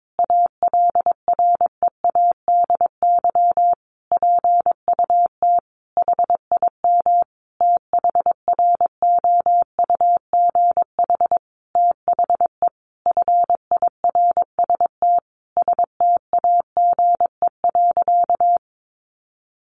Morse code, heard as a keyed tone, about 22 wpm, 700 Hz. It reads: ALREADY PUT HIM THROUGH THE FIRST STAGE.
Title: Sentences from Top 1200 Words 22wpm